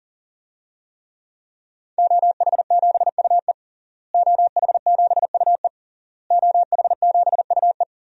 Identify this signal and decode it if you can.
Morse code, heard as a keyed tone, about 40 wpm, 700 Hz. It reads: OH7UE OH7UE OH7UE
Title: DXCC-100 Call Signs - 5 Characters - Repeated 3x - 40wpm